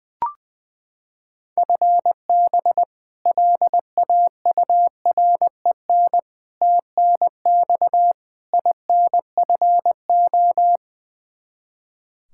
Morse code, heard as a keyed tone, about 20 wpm, 700 Hz. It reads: FB LAUREN TNX INFO
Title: K1USN SST QSO Practice 20wpm